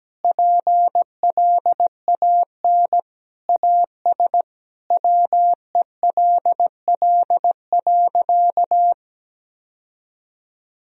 Morse code, heard as a keyed tone, about 17 wpm, 700 Hz. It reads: PLAN AS WELL.
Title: Sentences from Top 1000 Words - Review Part 2 - 17wpm